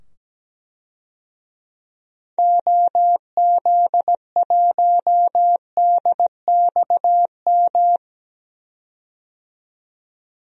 Morse code, heard as a keyed tone, about 17 wpm, 700 Hz. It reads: OZ1DXM
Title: DXCC-100 Call Signs - 6 Characters - 17wpm